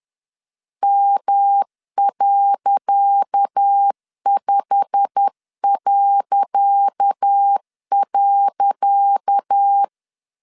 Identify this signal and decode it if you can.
Morse code, heard as a keyed tone, about 10 wpm, 800 Hz. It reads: M.5..
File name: LEZIONI / 10 SIGNS / 10C DOT 10wpm 2940